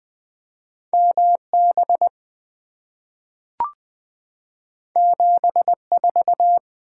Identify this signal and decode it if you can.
Morse code, heard as a keyed tone, about 20 wpm, 700 Hz. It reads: M B 7 4
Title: Two Letter-Number Combinations - 20wpm